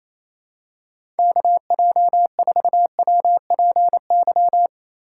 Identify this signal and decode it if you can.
Morse code, heard as a keyed tone, about 28 wpm, 700 Hz. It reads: KJ4WPY
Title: USA Call Signs - 6 Characters - 3x Character Spacing - 28wpm